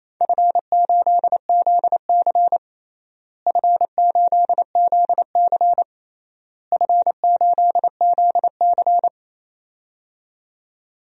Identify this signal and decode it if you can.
Morse code, heard as a keyed tone, about 27 wpm, 700 Hz. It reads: F8ZC F8ZC F8ZC
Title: DXCC-100 Call Signs - 4 Characters - Repeated 3x - 28wpm